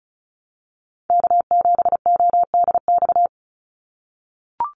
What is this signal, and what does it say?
Morse code, heard as a keyed tone, about 34 wpm, 700 Hz. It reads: K7ODX
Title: NA SOTA Activator Callsigns 35wpm